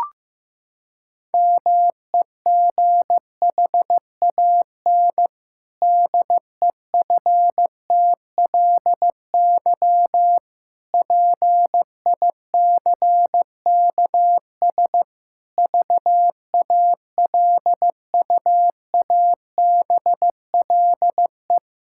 Morse code, heard as a keyed tone, about 15 wpm, 700 Hz. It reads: MEGHAN DEFTLY PICKS VALUABLE
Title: Pangram Challenge v2 - 15wpm